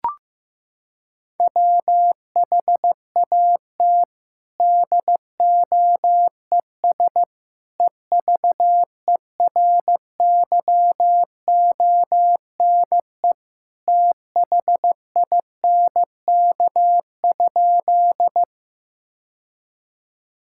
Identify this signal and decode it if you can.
Morse code, heard as a keyed tone, about 15 wpm, 700 Hz. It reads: WHAT DOES EVERYONE THINK?
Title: Sentences from Top 1000 Words - Review Part 5 - 15wpm